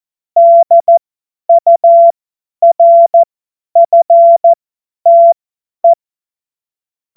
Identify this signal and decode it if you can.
Morse code, heard as a keyed tone, about 14 wpm, spaced as for 10 wpm, 680 Hz. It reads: DURFTE
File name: Morsekurs / Witze / Witz 03 Juni 2025 50 F70